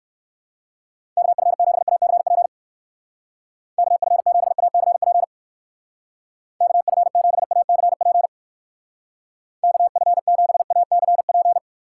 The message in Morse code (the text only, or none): KF7ACP KF7ACP KF7ACP KF7ACP